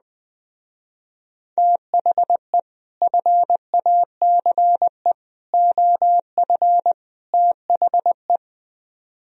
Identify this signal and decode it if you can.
Morse code, heard as a keyed tone, about 21 wpm, 700 Hz. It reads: THE FACE OF THE